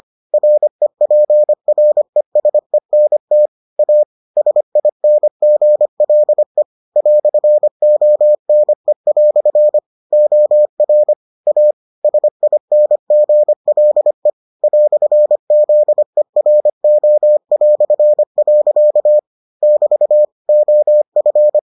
Morse code, heard as a keyed tone, about 25 wpm, 600 Hz. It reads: REPRESENT A SINGLE "ONE" OR A SINGLE "ZERO". <BT> OF